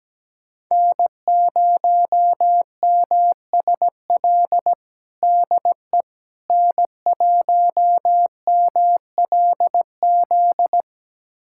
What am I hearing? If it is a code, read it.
Morse code, heard as a keyed tone, about 17 wpm, 700 Hz. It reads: N0MSL DE N1MLZ
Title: USA Callsign DE Callsign - 5 Characters - 17wpm